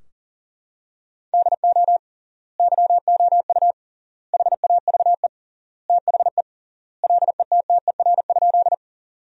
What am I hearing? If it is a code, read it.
Morse code, heard as a keyed tone, about 40 wpm, 700 Hz. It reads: DO YOU HAVE THE LETTER?
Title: 3x Word-Spacing - Sentences from Top 300 Words - 40wpm